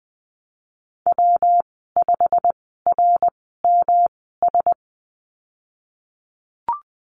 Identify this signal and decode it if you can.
Morse code, heard as a keyed tone, about 20 wpm, spaced as for 15 wpm, 700 Hz. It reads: W5RMS
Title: 128 Lesson 21g - Number 6 - Call Signs 20wpm